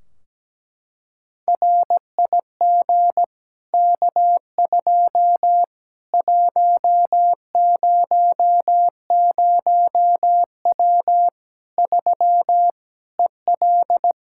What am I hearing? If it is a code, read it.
Morse code, heard as a keyed tone, about 17 wpm, 700 Hz. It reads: RIG K2 100W 3 EL
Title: K1USN SST QSO Practice 17wpm